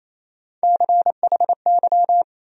Morse code, heard as a keyed tone, about 28 wpm, 700 Hz. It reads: CHY
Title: USA Call Sign Postfixes - 3 Letters 28wpm